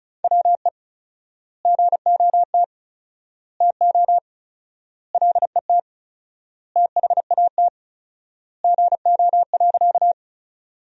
Morse code, heard as a keyed tone, about 34 wpm, 700 Hz. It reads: WE GOT TO LET THAT GO.
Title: 4x Word-Spacing - Sentences from Top 300 Words - 35wpm